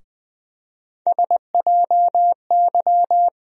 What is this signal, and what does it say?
Morse code, heard as a keyed tone, about 20 wpm, 700 Hz. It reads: SJY
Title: USA Call Sign Postfixes - 3 Letters 20wpm